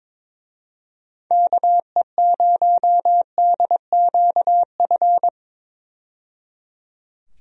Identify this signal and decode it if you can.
Morse code, heard as a keyed tone, about 21 wpm, 700 Hz. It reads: KE0DQF
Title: USA Call Signs - 6 Characters - 22wpm